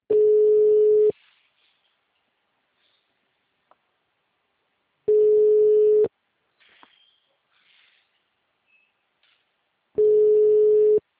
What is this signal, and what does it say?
Ringback tone (Europe, 425 Hz).